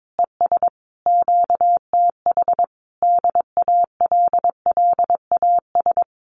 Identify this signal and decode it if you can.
Morse code, heard as a keyed tone, about 22 wpm, 700 Hz. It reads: ES QTH DALLAS
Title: Common QSO Phrases Encore 22wpm